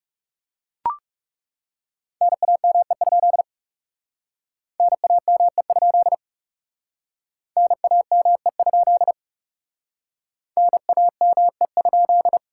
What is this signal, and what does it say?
Morse code, heard as a keyed tone, about 39 wpm, 700 Hz. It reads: NAME? NAME? NAME? NAME?